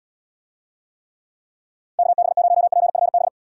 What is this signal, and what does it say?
Morse code, heard as a keyed tone, about 76 wpm, 700 Hz. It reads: KB0PFB